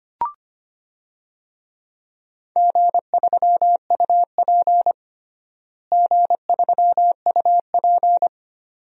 Morse code, heard as a keyed tone, about 25 wpm, 700 Hz. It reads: G3UP G3UP
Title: DXCC-100 Call Signs - 4 Characters - Repeated 2x - 25wpm